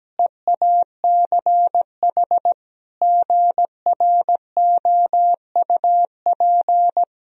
Morse code, heard as a keyed tone, about 17 wpm, 700 Hz. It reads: EACH GROUP
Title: Sentences from Top 1000 Words - Review Part 1 - 17wpm